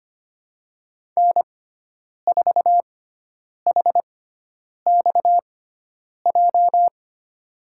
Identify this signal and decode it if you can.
Morse code, heard as a keyed tone, about 25 wpm, spaced as for 9 wpm, 700 Hz. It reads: N4HXJ
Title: USA Call Signs - 5 Characters - 6x Character Spacing - 25wpm